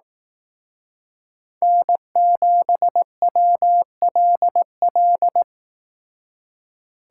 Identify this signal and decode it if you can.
Morse code, heard as a keyed tone, about 18 wpm, 700 Hz. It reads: N7WLL
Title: Speed Racing - USA 1x3 Call Signs 15wpm